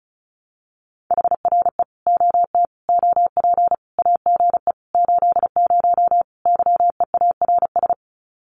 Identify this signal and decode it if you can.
Morse code, heard as a keyed tone, about 34 wpm, 700 Hz. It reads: HRE OT OP AGE 80 YEARS
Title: Common QSO Phrases 35wpm